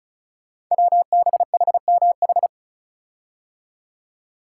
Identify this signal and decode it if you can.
Morse code, heard as a keyed tone, about 36 wpm, 700 Hz. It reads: WBHMH
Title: Code Groups - 5 Letter:Numbers - 35wpm